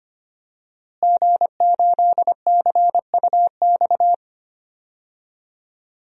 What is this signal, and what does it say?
Morse code, heard as a keyed tone, about 25 wpm, 700 Hz. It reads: G8CUX
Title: DXCC-100 Call Signs - 5 Characters - Repeated 2x - 25wpm